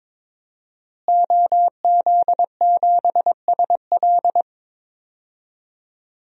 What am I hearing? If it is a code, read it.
Morse code, heard as a keyed tone, about 22 wpm, 700 Hz. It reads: OZ7SL
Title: DXCC-100 Call Signs - 5 Characters - Repeated 2x - 22wpm